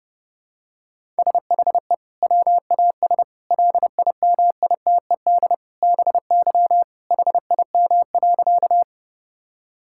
Morse code, heard as a keyed tone, about 28 wpm, 700 Hz. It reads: SHE WAS LIMITED BY HIM.
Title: Sentences from Top 1000 Words - Review Part 6 - 30wpm